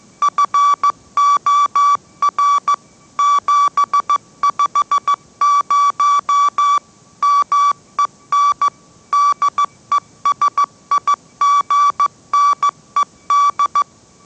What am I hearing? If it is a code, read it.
Morse code, heard as a keyed tone, about 15 wpm, 1150 Hz. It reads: FOR 750 MEN DESIGNED
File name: mrx / mrx065 16wpm